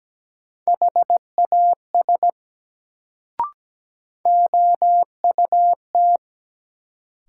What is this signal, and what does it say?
Morse code, heard as a keyed tone, about 17 wpm, 700 Hz. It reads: HAS OUT
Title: Sentences from Top 100 Words - New Word Drill - 17wpm